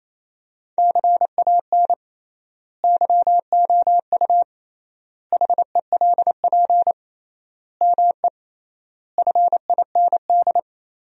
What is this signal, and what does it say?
Morse code, heard as a keyed tone, about 28 wpm, 700 Hz. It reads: CAN YOU HELP ME FIND
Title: 3x Word-Spacing - Sentences from Top 200 Words - 28wpm